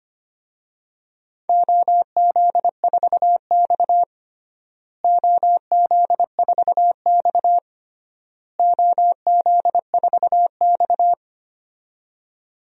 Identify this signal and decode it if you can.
Morse code, heard as a keyed tone, about 25 wpm, 700 Hz. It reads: OZ4X OZ4X OZ4X
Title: DXCC-100 Call Signs - 4 Characters - Repeated 3x - 25wpm